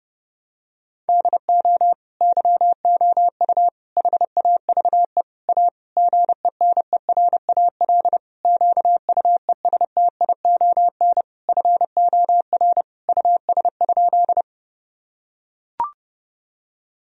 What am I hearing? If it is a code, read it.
Morse code, heard as a keyed tone, about 30 wpm, 700 Hz. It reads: DO YOU HAVE A GENERAL QUESTION FOR US?
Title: Sentences from Top 1000 Words - Review Part 3 - 30wpm